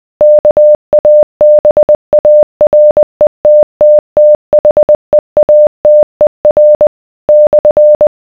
Morse code, heard as a keyed tone, about 20 wpm, 600 Hz. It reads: KABARETTTHEATER /